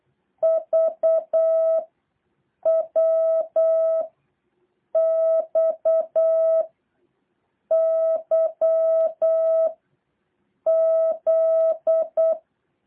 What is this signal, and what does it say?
Morse code, heard as a keyed tone, about 8 wpm, 650 Hz. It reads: V W X Y Z
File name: jh8cbh / 2023cw